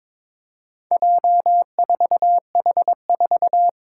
Morse code, heard as a keyed tone, about 22 wpm, 700 Hz. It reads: J4H4
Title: Four Challenging Letter-Number Combinations 22wpm